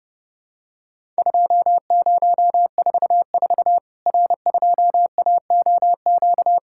Morse code, heard as a keyed tone, about 30 wpm, 700 Hz. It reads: 2044 R2AOQ
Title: Worked All Europe DX Contest - QTC Exchange - 30wpm